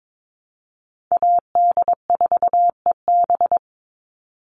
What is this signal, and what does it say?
Morse code, heard as a keyed tone, about 22 wpm, 700 Hz. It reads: AD4EB
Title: Call Signs - 22wpm